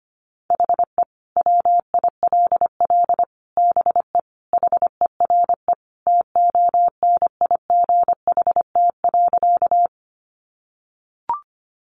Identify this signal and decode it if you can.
Morse code, heard as a keyed tone, about 25 wpm, 700 Hz. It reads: HE WILL BE HERE TONIGHT.
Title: Sentences from Top 600 Words 25wpm